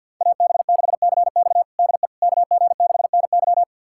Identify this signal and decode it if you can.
Morse code, heard as a keyed tone, about 52 wpm, 700 Hz. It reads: AB6CX DE KG6NY
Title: USA Callsign DE Callsign - 5 Characters - 50wpm